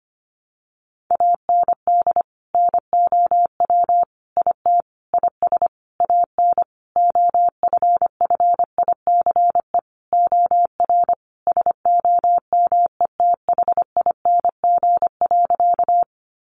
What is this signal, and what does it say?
Morse code, heard as a keyed tone, about 25 wpm, 700 Hz. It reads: AND NOW IT IS AN OFFICE OR SOMETHING.
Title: Sentences from Top 1000 Words 25wpm